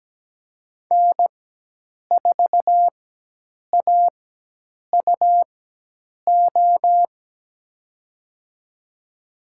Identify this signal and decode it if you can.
Morse code, heard as a keyed tone, about 17 wpm, spaced as for 8 wpm, 700 Hz. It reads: N4AUO
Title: USA Call Signs - 5 Characters - 4x Character Spacing - 17wpm